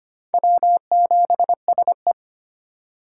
Morse code, heard as a keyed tone, about 25 wpm, 700 Hz. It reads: W7SE
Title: K1USN SST QSO Practice - 25wpm